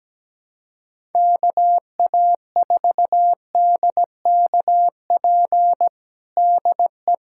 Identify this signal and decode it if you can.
Morse code, heard as a keyed tone, about 17 wpm, 700 Hz. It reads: KA4DKP DE
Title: USA Callsign DE Callsign 17wpm